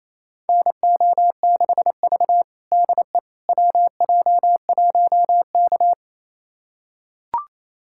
Morse code, heard as a keyed tone, about 28 wpm, 700 Hz. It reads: NO6V DE WJ1K
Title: USA Callsign DE Callsign - 4 Characters - 28wpm